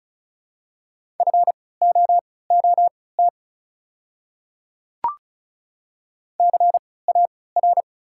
Morse code, heard as a keyed tone, about 34 wpm, spaced as for 20 wpm, 690 Hz. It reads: FOOT CAR
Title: 3x Character Spacing - Top 500 Words - 35wpm